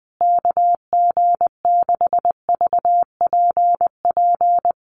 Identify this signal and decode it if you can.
Morse code, heard as a keyed tone, about 19 wpm, 700 Hz. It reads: KG6VPP
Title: USA Call Signs Repeated 3x 20wpm